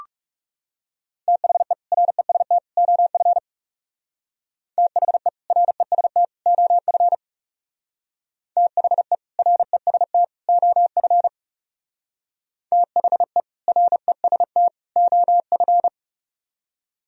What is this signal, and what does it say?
Morse code, heard as a keyed tone, about 36 wpm, 700 Hz. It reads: THE REST OF THE REST OF THE REST OF THE REST OF
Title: Speed-Racing - Sets of 3 Words 30wpm